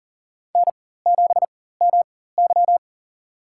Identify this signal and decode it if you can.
Morse code, heard as a keyed tone, about 37 wpm, spaced as for 18 wpm, 700 Hz. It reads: N7MY
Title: USA Call Signs - 4 Characters - 4x Character Spacing - 40wpm